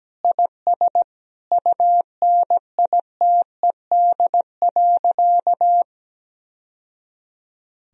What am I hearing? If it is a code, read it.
Morse code, heard as a keyed tone, about 17 wpm, 700 Hz. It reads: IS UNITED.
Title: Sentences from Top 1000 Words - Review Part 4 - 17wpm